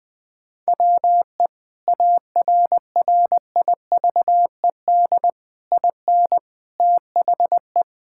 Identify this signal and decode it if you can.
Morse code, heard as a keyed tone, about 20 wpm, 700 Hz. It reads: WE ARRIVED IN THE
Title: Sentences with 8-Word Phrases - 20wpm